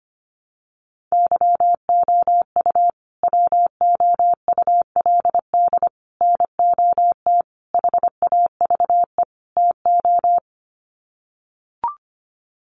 Morse code, heard as a keyed tone, about 25 wpm, 700 Hz. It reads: YOU WOULD NOT HAVE TO
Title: Sets of 5 Words Encore 25wpm